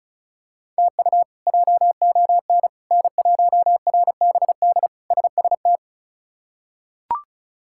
Morse code, heard as a keyed tone, about 36 wpm, 700 Hz. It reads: TU JON N1RBD SST
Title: K1USN SST QSO Practice - 35wpm